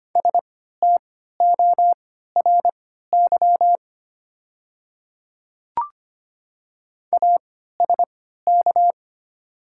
Morse code, heard as a keyed tone, about 25 wpm, spaced as for 14 wpm, 700 Hz. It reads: STORY ASK
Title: 3x Character Spacing - Top 300 Words - 25wpm